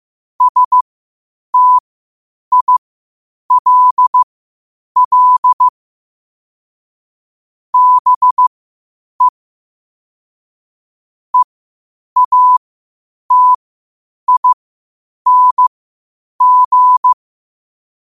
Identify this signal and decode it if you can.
Morse code, heard as a keyed tone, about 15 wpm, spaced as for 8 wpm, 1000 Hz. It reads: STILL BE EATING
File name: qotdfiles / 8wpm